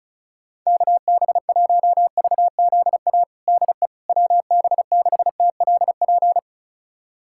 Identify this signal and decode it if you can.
Morse code, heard as a keyed tone, about 36 wpm, 700 Hz. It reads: KB1VZA DE WB6TLP
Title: USA Callsign DE Callsign - 6 Characters - 35wpm